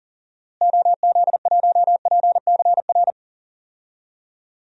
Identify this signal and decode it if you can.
Morse code, heard as a keyed tone, about 40 wpm, 700 Hz. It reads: OZ1PCR
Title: DXCC-100 Call Signs - 6 Characters - Repeated 2x - 40wpm